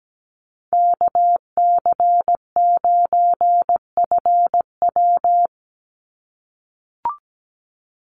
Morse code, heard as a keyed tone, about 17 wpm, 700 Hz. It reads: KC9FW
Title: USA Call Signs 17wpm